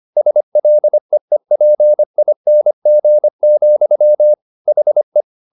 Morse code, heard as a keyed tone, about 25 wpm, 600 Hz. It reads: SLEEPING, HE